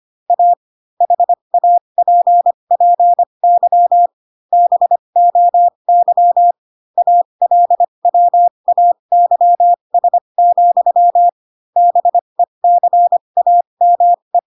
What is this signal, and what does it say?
Morse code, heard as a keyed tone, about 25 wpm, spaced as for 22 wpm, 700 Hz. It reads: A HAPPY BOY ALWAYS, BECAME